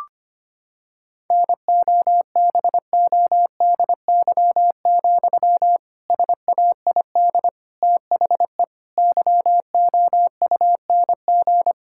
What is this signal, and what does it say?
Morse code, heard as a keyed tone, about 25 wpm, 700 Hz. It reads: NOBODY, SAID THE YOUNG
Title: Sentences from Top 1600 Words 25wpm